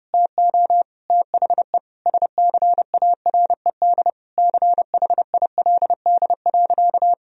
Morse code, heard as a keyed tone, about 30 wpm, 700 Hz. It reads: TO THE SCARED CHILD.
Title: Sentences from Top 1000 Words - Review Part 4 - 30wpm